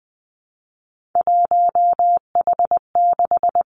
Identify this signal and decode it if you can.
Morse code, heard as a keyed tone, about 20 wpm, 700 Hz. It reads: 1H6
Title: Three Challenging Letter-Number Combinations 20wpm